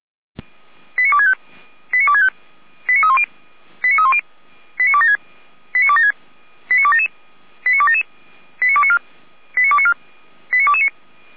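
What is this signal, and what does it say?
Five-tone selective call (ZVEI), heard as quick runs of five tones, tones of about 70 ms. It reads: 89265 89265 89310 89310 89276 89276 89277 89277 89294 89294 89229
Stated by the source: ZVEI five-tone selective call received by SDR, from the feuerwehr-orlamuende.org fire brigade archive.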